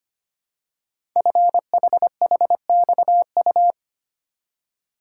Morse code, heard as a keyed tone, about 25 wpm, 700 Hz. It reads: FHHXU
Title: Code Groups - 5 Letters - 25wpm